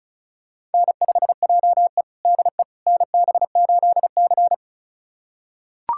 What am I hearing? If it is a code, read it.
Morse code, heard as a keyed tone, about 36 wpm, 700 Hz. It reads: N5JE DE NB8C
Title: USA Callsign DE Callsign - 4 Characters - 35wpm